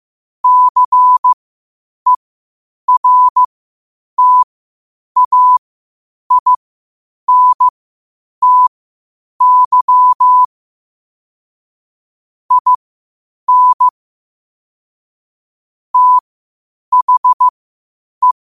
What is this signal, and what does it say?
Morse code, heard as a keyed tone, about 15 wpm, spaced as for 8 wpm, 1000 Hz. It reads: CERTAINTY IN THE
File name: qotdfiles / 8wpm